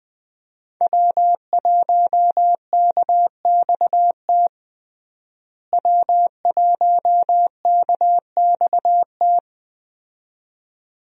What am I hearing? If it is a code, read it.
Morse code, heard as a keyed tone, about 21 wpm, 700 Hz. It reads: W1KXT W1KXT
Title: USA Call Signs - 5 Characters - Repeated 2x - 20wpm